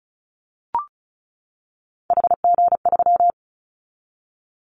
Morse code, heard as a keyed tone, about 34 wpm, 700 Hz. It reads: HG3
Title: Callsign Prefixes for Top 25 DXCC - 35wpm